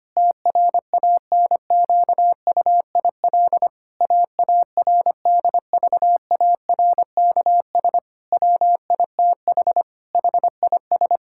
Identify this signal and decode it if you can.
Morse code, heard as a keyed tone, about 25 wpm, 700 Hz. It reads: TRANQUIL AARDVARKS WITH HIS
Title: Pangram Challenge v2 - 25wpm